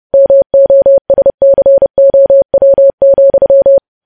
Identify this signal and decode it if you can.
Morse code, heard as a keyed tone, about 30 wpm, 550 Hz. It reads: MOSCOW,
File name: podcast / News headlines-1x-30x30